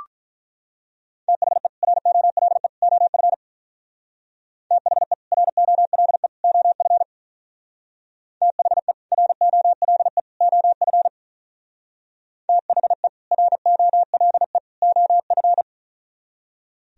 Morse code, heard as a keyed tone, about 43 wpm, 700 Hz. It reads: THE ROLE OF THE ROLE OF THE ROLE OF THE ROLE OF
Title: Speed-Racing - Sets of 3 Words 35wpm